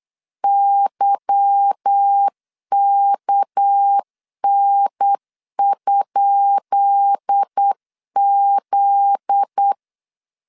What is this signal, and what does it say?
Morse code, heard as a keyed tone, about 8 wpm, 800 Hz. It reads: YKN?Z
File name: LEZIONI / 10 SIGNS / 10B INT 8wpm 3534